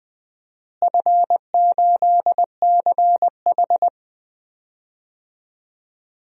Morse code, heard as a keyed tone, about 21 wpm, 700 Hz. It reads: F8CH
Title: DXCC-100 Call Signs - 4 Characters - Repeated 2x - 20wpm